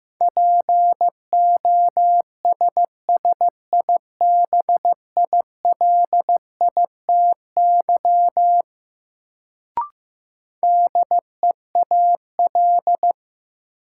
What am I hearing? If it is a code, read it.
Morse code, heard as a keyed tone, about 15 wpm, 700 Hz. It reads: POSSIBILITY DEAL
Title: Sets of 3 Words - New Word Drill - 15wpm